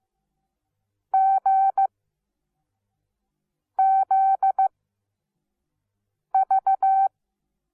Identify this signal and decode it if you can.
Morse code, heard as a keyed tone, about 15 wpm, spaced as for 4 wpm, 770 Hz. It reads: GZV
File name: Onlinelehrgang / morsen / DJ4UF Morsen17